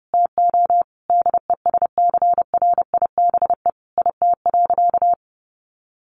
Morse code, heard as a keyed tone, about 30 wpm, 700 Hz. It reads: TO DESCRIBE IT.
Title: Sentences from Top 1100 Words 30wpm